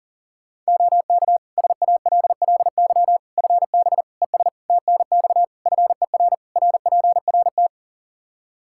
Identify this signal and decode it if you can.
Morse code, heard as a keyed tone, about 40 wpm, 700 Hz. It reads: OK SALLY FB ES TNX FER RPRT
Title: Practice QSO - Exchange 2 - Info - 40wpm